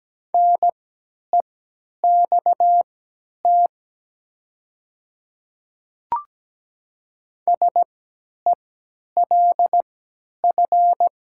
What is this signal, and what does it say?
Morse code, heard as a keyed tone, about 17 wpm, spaced as for 10 wpm, 700 Hz. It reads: NEXT SELF
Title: 3x Character Spacing - Top 300 Words - 17wpm